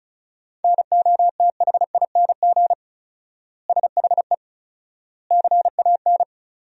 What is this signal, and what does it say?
Morse code, heard as a keyed tone, about 34 wpm, 700 Hz. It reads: NOTHING SHE CAN
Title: 4x Word-Spacing - Sentences from Top 500 Words - 35wpm